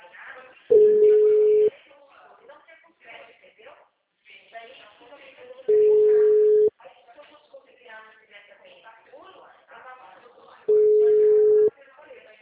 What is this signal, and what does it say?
Ringback tone (Europe, 425 Hz), with speech.